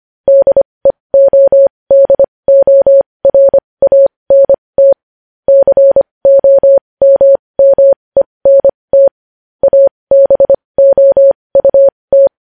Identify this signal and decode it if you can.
Morse code, heard as a keyed tone, about 25 wpm, spaced as for 20 wpm, 550 Hz. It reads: DEODORANT COMMENT ABOUT
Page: news-headlines-in-morse-code-at-20-wpm